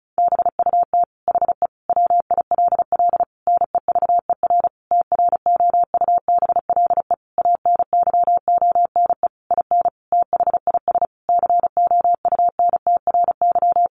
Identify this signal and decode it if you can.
Morse code, heard as a keyed tone, about 34 wpm, 700 Hz. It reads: BUT HE WILL NEVER TROUBLE ANYONE IN THIS COUNTRY
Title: Sentences from Top 1200 Words 35wpm